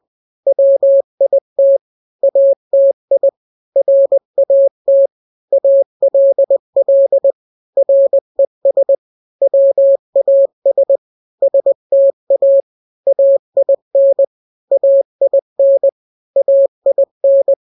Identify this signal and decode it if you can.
Morse code, heard as a keyed tone, about 20 wpm, 550 Hz. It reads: WIT ATI RAT ALL RES WAS STA AIN AIN AIN
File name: n-grams / 3-grams 20 18 WPM